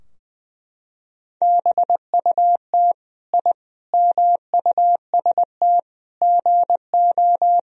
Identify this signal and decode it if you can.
Morse code, heard as a keyed tone, about 20 wpm, 700 Hz. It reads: BUT I MUST GO
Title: Sentences from Top 1000 Words - Review Part 3 - 20wpm